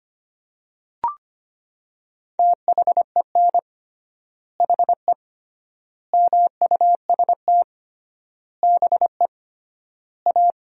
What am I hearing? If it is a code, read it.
Morse code, heard as a keyed tone, about 25 wpm, 700 Hz. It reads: THEN HE MUST BE A
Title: 3x Word-Spacing - Sentences from Top 500 Words - 25wpm